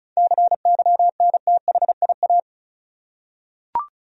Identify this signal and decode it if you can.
Morse code, heard as a keyed tone, about 36 wpm, 700 Hz. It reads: CYNTHIA
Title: USA Common Names - 35wpm